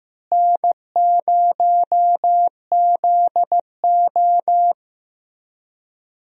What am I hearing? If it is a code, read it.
Morse code, heard as a keyed tone, about 15 wpm, 700 Hz. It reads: N0ZO
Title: USA Call Signs - 4 Characters - Repeated 3x - 15wpm